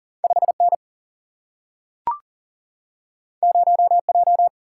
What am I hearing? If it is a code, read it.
Morse code, heard as a keyed tone, about 41 wpm, 700 Hz. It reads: 5N 0J
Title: Single-Letter Single-Number 40wpm